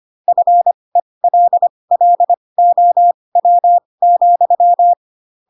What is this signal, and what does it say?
Morse code, heard as a keyed tone, about 25 wpm, spaced as for 20 wpm, 700 Hz. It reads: FELLOW,
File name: Pinocchio / Pinocchio ch6 15 25 ik1hge